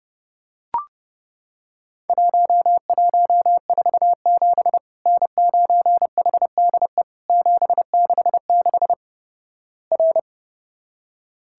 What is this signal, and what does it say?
Morse code, heard as a keyed tone, about 28 wpm, 700 Hz. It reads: 1147 N9HDE 766 R
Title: Worked All Europe DX Contest - QTC Exchange - 30wpm